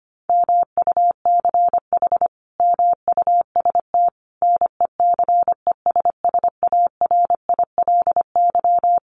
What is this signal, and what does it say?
Morse code, heard as a keyed tone, about 25 wpm, 700 Hz. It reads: MUCH MUST NECESSARILY
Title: Sentences from Top 1400 Words 25wpm